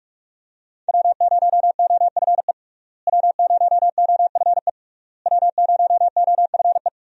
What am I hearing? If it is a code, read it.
Morse code, heard as a keyed tone, about 44 wpm, 700 Hz. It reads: W0OFE W0OFE W0OFE
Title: USA Call Signs - 5 Characters - Repeated 3x - 45wpm